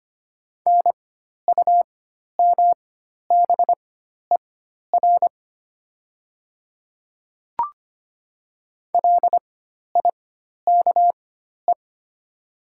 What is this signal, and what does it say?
Morse code, heard as a keyed tone, about 25 wpm, spaced as for 12 wpm, 700 Hz. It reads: NUMBER LIKE
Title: 4x Character Spacing - Top 200 Words - 25wpm